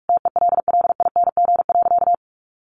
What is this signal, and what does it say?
Morse code, heard as a keyed tone, about 44 wpm, 700 Hz. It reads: TELLING.
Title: A Sound Of Thunder 45wpm